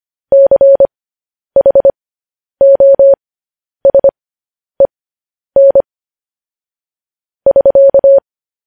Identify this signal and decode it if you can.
Morse code, heard as a keyed tone, about 25 wpm, spaced as for 10 wpm, 550 Hz. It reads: CHOSEN <SK>
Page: news-headlines-in-morse-code-at-10-wpm